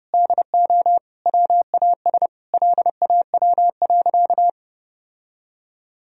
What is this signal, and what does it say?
Morse code, heard as a keyed tone, about 30 wpm, 700 Hz. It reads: DO WAS LAW.
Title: Sentences from Top 1000 Words - Review Part 3 - 30wpm